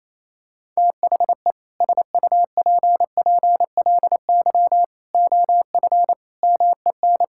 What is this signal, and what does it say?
Morse code, heard as a keyed tone, about 28 wpm, 700 Hz. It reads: THE SUPPLY OF MEN
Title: Sentences from Top 1000 Words - Review Part 6 - 28wpm